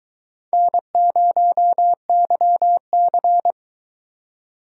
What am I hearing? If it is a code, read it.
Morse code, heard as a keyed tone, about 22 wpm, 700 Hz. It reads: N0YC